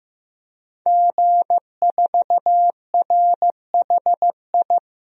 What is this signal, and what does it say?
Morse code, heard as a keyed tone, about 15 wpm, 700 Hz. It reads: G4RHI
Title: DXCC-100 Call Signs - 5 Characters - 15wpm